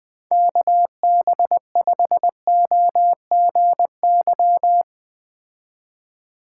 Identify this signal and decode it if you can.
Morse code, heard as a keyed tone, about 20 wpm, 700 Hz. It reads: KB5OGY
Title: Speed Racing - USA Call Signs 17wpm